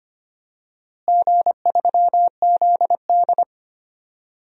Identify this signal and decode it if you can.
Morse code, heard as a keyed tone, about 25 wpm, 700 Hz. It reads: G3ZD
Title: DXCC-100 Call Signs - 4 Characters - Repeated 3x - 25wpm